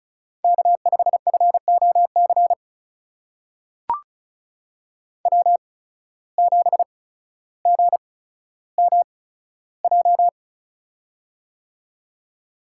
Morse code, heard as a keyed tone, about 36 wpm, spaced as for 15 wpm, 700 Hz. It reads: K5FOC W7GMJ
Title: USA Call Signs - 5 Characters - 8x Character Spacing - 35wpm